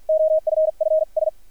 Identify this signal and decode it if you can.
Morse code, heard as a keyed tone, about 50 wpm, spaced as for 42 wpm, 630 Hz, with radio noise. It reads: OFFS
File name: archived / CRSnet